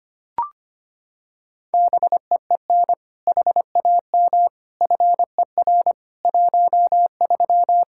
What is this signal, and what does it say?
Morse code, heard as a keyed tone, about 25 wpm, 700 Hz. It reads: BEEN HAM FER 13
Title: Practice QSO - Exchange 3 - More Info - 25wpm